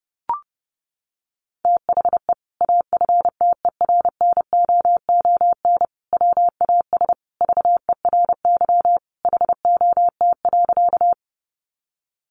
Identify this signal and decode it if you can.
Morse code, heard as a keyed tone, about 30 wpm, 700 Hz. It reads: THE AFTERNOON WAS VERY HOT.
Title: Sentences from Top 1500 Words 30wpm